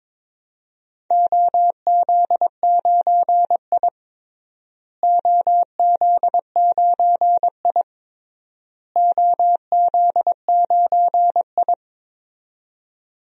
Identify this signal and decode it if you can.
Morse code, heard as a keyed tone, about 22 wpm, 700 Hz. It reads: OZ9I OZ9I OZ9I
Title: DXCC-100 Call Signs - 4 Characters - Repeated 3x - 22wpm